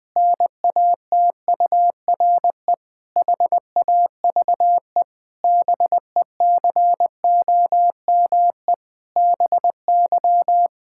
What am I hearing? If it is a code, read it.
Morse code, heard as a keyed tone, about 21 wpm, 700 Hz. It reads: NATURE HAVE BECOME BY
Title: Pangram Challenge v2 - 20wpm